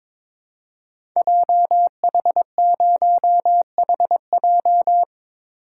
Morse code, heard as a keyed tone, about 22 wpm, 700 Hz. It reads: JH0HJ
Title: Code Groups - 5 Letter:Numbers - 22wpm